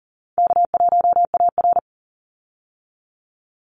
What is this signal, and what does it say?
Morse code, heard as a keyed tone, about 40 wpm, 700 Hz. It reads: K1AR
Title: Easy Call Signs Repeated 3x - 40wpm